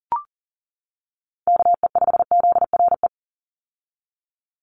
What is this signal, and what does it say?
Morse code, heard as a keyed tone, about 40 wpm, 700 Hz. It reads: KE5ZRE
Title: USA Call Signs 40wpm